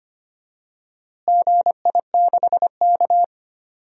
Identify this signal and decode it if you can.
Morse code, heard as a keyed tone, about 25 wpm, 700 Hz. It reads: GI6K
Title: DXCC-100 Call Signs - 4 Characters - Repeated 2x - 25wpm